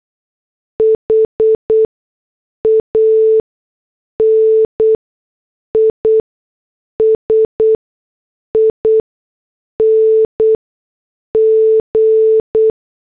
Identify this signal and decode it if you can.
Morse code, heard as a keyed tone, about 8 wpm, spaced as for 6 wpm, 440 Hz. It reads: HANISING